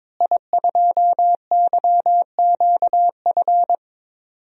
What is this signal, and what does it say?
Morse code, heard as a keyed tone, about 22 wpm, 700 Hz. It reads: I2YQF
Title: DXCC-100 Call Signs - 5 Characters - Repeated 3x - 22wpm